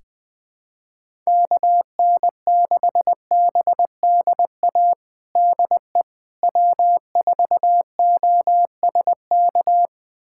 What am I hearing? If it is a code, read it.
Morse code, heard as a keyed tone, about 20 wpm, 700 Hz. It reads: KN6BDA DE W4OSK
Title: USA Callsign DE Callsign 20wpm